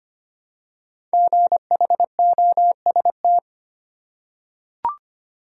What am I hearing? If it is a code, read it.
Morse code, heard as a keyed tone, about 25 wpm, 700 Hz. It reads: GHOST
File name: 5 Letter Words - First 3 Letters Given - 25wpm